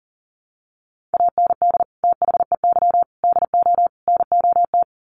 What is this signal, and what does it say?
Morse code, heard as a keyed tone, about 37 wpm, 700 Hz. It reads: AND THEY DO NOT
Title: Sets of 4 Words 40wpm